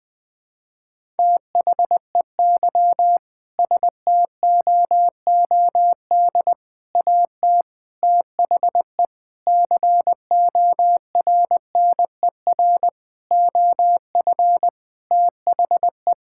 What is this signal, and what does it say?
Morse code, heard as a keyed tone, about 20 wpm, 700 Hz. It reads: THEY STOOD AT THE CORNER OF THE
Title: Sentences from Top 1600 Words 20wpm